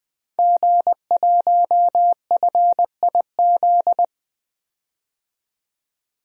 Morse code, heard as a keyed tone, about 20 wpm, 700 Hz. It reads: G1FIZ
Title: DXCC-100 Call Signs - 5 Characters - 20wpm